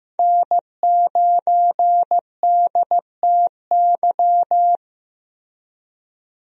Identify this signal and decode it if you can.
Morse code, heard as a keyed tone, about 15 wpm, 700 Hz. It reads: N9DTY
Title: USA Call Signs - 5 Characters - Repeated 3x - 15wpm